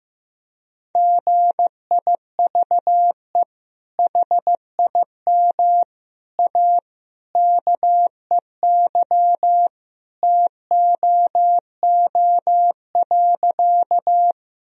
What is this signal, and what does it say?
Morse code, heard as a keyed tone, about 15 wpm, 700 Hz. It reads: GIVE HIM A KEY TOO.
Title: Sentences from Top 1000 Words - Review Part 4 - 15wpm